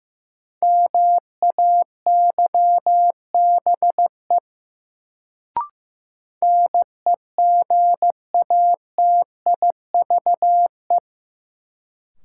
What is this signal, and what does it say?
Morse code, heard as a keyed tone, about 16 wpm, 700 Hz. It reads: MAYBE NEGATIVE
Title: Sentences from Top 1000 Words - New Word Drill - 15wpm